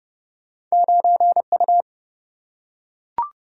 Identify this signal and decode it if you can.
Morse code, heard as a keyed tone, about 30 wpm, 700 Hz. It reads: 9U
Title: Single-Letter Single-Number 30wpm